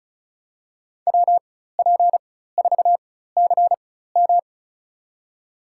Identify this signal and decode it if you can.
Morse code, heard as a keyed tone, about 36 wpm, spaced as for 16 wpm, 700 Hz. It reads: WP4CM
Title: USA Call Signs - 5 Characters - 4x Character Spacing - 35wpm